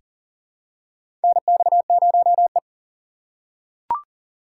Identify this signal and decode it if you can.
Morse code, heard as a keyed tone, about 40 wpm, 700 Hz. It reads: NX0E
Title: USA Call Signs - 4 Characters - 4x Character Spacing - 40wpm